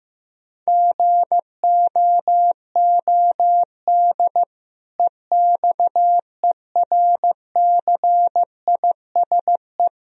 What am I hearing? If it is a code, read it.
Morse code, heard as a keyed tone, about 15 wpm, 700 Hz. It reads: GOOD EXERCISE
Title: Sentences from Top 1000 Words - Review Part 4 - 15wpm